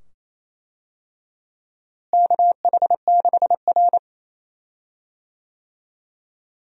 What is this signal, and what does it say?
Morse code, heard as a keyed tone, about 28 wpm, 700 Hz. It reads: KH6R
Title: DXCC-100 Call Signs - 4 Characters - Repeated 2x - 28wpm